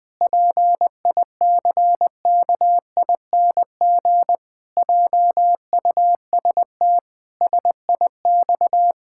Morse code, heard as a keyed tone, about 21 wpm, 700 Hz. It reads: PICKING JUST SIX
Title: Pangram Challenge v2 - 20wpm